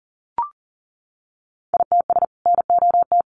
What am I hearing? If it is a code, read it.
Morse code, heard as a keyed tone, about 40 wpm, 700 Hz. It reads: ITS NOT
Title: Sentences from Top 600 Words 40wpm